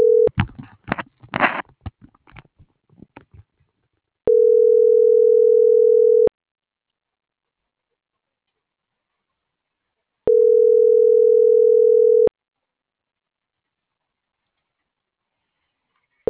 Ringback tone (North America), 440 + 480 Hz.